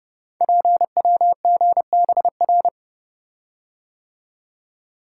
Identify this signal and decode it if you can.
Morse code, heard as a keyed tone, about 30 wpm, 700 Hz. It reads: PWGBR
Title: Code Groups - 5 Letters - 30wpm